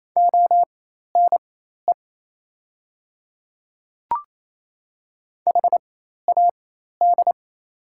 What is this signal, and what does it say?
Morse code, heard as a keyed tone, about 28 wpm, spaced as for 13 wpm, 700 Hz. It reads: ONE HAD
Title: 4x Character Spacing - Top 100 Words - 28wpm